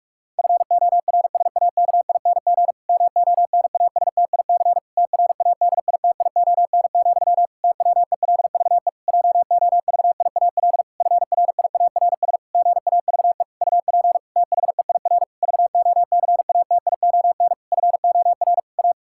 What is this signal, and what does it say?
Morse code, heard as a keyed tone, about 41 wpm, 700 Hz. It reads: FORSAKING MONASTIC TRADITION, TWELVE JOVIAL FRIARS GAVE UP THEIR VOCATION FOR A
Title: Pangram Challenge v2 - 45wpm